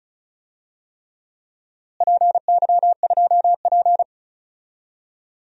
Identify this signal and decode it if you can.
Morse code, heard as a keyed tone, about 36 wpm, 700 Hz. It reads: PY2P
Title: DXCC-100 Call Signs - 4 Characters - Repeated 3x - 35wpm